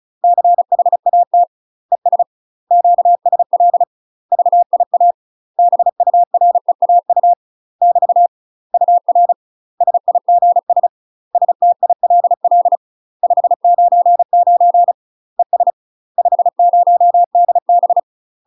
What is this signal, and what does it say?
Morse code, heard as a keyed tone, about 34 wpm, 700 Hz. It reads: CHAT ES QSL VIA BUREAU <BT> UR SIGS STILL 599 ES 50DB
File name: UK MASTER QSOs TRAINING / 35WPM / UK MASTER QSOs 0000